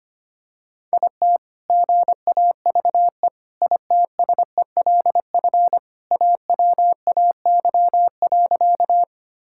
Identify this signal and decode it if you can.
Morse code, heard as a keyed tone, about 25 wpm, 700 Hz. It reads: IT GAVE ITSELF AWAY.
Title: Sentences from Top 1000 Words - Review Part 5 - 25wpm